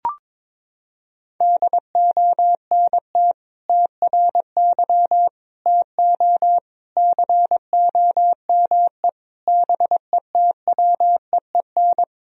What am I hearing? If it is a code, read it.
Morse code, heard as a keyed tone, about 22 wpm, 700 Hz. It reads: DONT TRY TO COME BETWEEN
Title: Sentences from Top 1000 Words - Review Part 1 - 22wpm